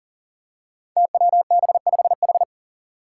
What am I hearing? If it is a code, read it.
Morse code, heard as a keyed tone, about 40 wpm, 700 Hz. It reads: TWB5H
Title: Five Challenging Letter-Number Combinations 40wpm